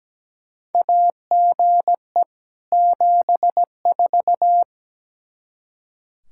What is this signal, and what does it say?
Morse code, heard as a keyed tone, about 17 wpm, 700 Hz. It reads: AGE 74
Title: Practice QSO - Exchange 3 - More Info - 17wpm